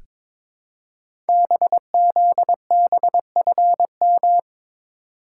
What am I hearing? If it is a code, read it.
Morse code, heard as a keyed tone, about 22 wpm, 700 Hz. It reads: BZBFM
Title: Code Groups - 5 Letters - 22wpm